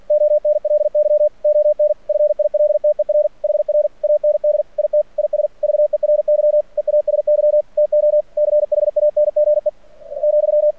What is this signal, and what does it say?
Morse code, heard as a keyed tone, about 46 wpm, 600 Hz. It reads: ONLY ON FILTER HR AND IT IS VERY EASY TO CHANGE ,
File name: archived / CRSnet